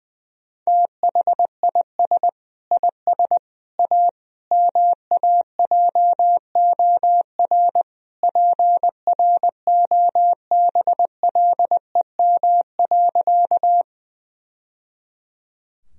Morse code, heard as a keyed tone, about 20 wpm, 700 Hz. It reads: THIS IS A MAJOR PROBLEM.
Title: Sentences from Top 1000 Words - Review Part 4 - 20wpm